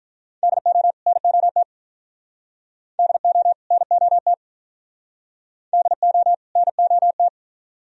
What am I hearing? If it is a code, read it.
Morse code, heard as a keyed tone, about 43 wpm, 700 Hz. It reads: DO NOT DO NOT DO NOT
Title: Speed-Racing - Sets of 2 Words 35wpm